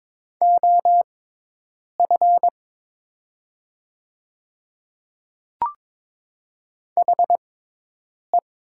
Morse code, heard as a keyed tone, about 22 wpm, spaced as for 8 wpm, 700 Hz. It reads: OF HE
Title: 6x Character Spacing - Top 100 Words - 22wpm